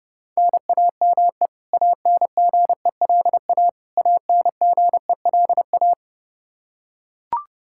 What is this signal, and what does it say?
Morse code, heard as a keyed tone, about 30 wpm, 700 Hz. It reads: NAME ANGELA ANGELA
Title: Practice QSO - Exchange 1 - Beginning - 30wpm